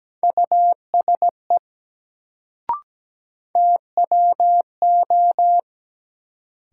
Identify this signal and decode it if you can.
Morse code, heard as a keyed tone, about 17 wpm, 700 Hz. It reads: USE TWO
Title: Sets of 2 Words - New Word Drill - 17wpm